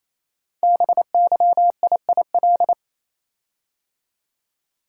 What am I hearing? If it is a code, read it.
Morse code, heard as a keyed tone, about 29 wpm, 700 Hz. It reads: BYIIL
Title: Code Groups - 5 Letters - 28wpm